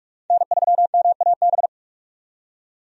Morse code, heard as a keyed tone, about 46 wpm, 700 Hz. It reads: N3MAB
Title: USA Call Signs - 5 Characters - 45wpm